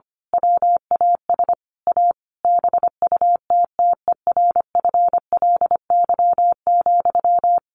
Morse code, heard as a keyed tone, about 25 wpm, 700 Hz. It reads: WAS A BUTTERFLY,
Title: A Sound Of Thunder 25wpm